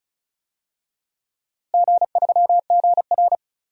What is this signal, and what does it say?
Morse code, heard as a keyed tone, about 36 wpm, 700 Hz. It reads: G3GR
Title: DXCC-100 Call Signs - 4 Characters - Repeated 2x - 35wpm